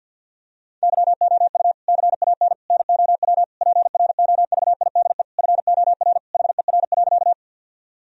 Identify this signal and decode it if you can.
Morse code, heard as a keyed tone, about 50 wpm, 700 Hz. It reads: YOU CAN NOW PROVIDE FOR HER.
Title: Sentences from Top 1000 Words - Review Part 3 - 50wpm